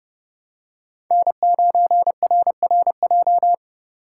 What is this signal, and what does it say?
Morse code, heard as a keyed tone, about 30 wpm, 700 Hz. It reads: N9RRJ
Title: USA Call Signs - 3x Character Spacing - 30wpm